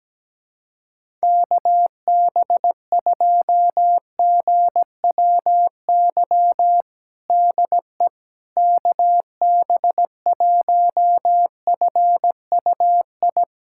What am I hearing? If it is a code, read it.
Morse code, heard as a keyed tone, about 17 wpm, 700 Hz. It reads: KB2GWY DE KB1FUI
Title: USA Callsign DE Callsign - 6 Characters - 17wpm